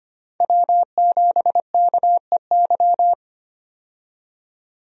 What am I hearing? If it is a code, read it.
Morse code, heard as a keyed tone, about 25 wpm, 700 Hz. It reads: W7KEY
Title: USA Call Signs - 5 Characters - 25wpm